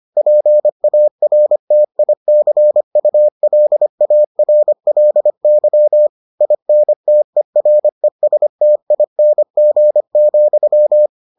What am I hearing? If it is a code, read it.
Morse code, heard as a keyed tone, about 25 wpm, 600 Hz. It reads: PARTICULARLY INTERESTING,